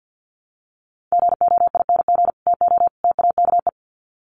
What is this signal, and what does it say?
Morse code, heard as a keyed tone, about 50 wpm, 700 Hz. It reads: GOING TO TAKE
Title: Sets of 3 Words 50wpm